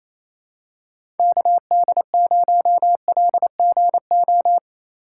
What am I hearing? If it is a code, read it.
Morse code, heard as a keyed tone, about 28 wpm, 700 Hz. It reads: KD0LGO
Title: USA Call Signs - 4x Character Spacing - 28wpm